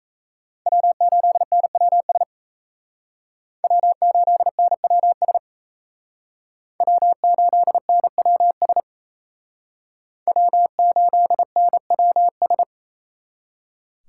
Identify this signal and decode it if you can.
Morse code, heard as a keyed tone, about 37 wpm, 700 Hz. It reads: W8NWS W8NWS W8NWS W8NWS